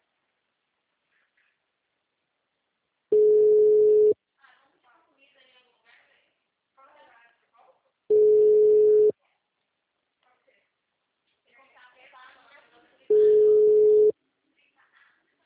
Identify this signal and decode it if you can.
Ringback tone (Europe, 425 Hz), with speech.
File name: out- -2064- - - .25912